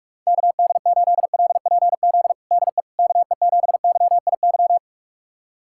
Morse code, heard as a keyed tone, about 46 wpm, 700 Hz. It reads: KD8LPZ DE KE7YIY
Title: USA Callsign DE Callsign - 6 Characters - 45wpm